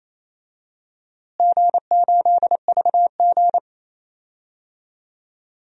Morse code, heard as a keyed tone, about 27 wpm, 700 Hz. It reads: G8VG
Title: DXCC-100 Call Signs - 4 Characters - Repeated 3x - 28wpm